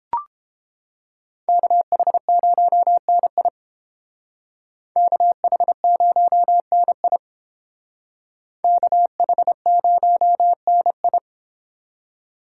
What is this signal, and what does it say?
Morse code, heard as a keyed tone, about 29 wpm, 700 Hz. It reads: KH0NI KH0NI KH0NI